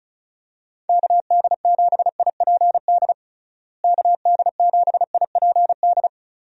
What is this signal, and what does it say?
Morse code, heard as a keyed tone, about 36 wpm, 700 Hz. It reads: KD7IPD KD7IPD
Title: USA Call Signs - 6 Characters - Repeated 2x - 35wpm